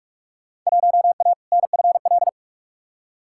Morse code, heard as a keyed tone, about 43 wpm, 700 Hz. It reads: 1A NFL
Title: Field Day Exchanges - 45wpm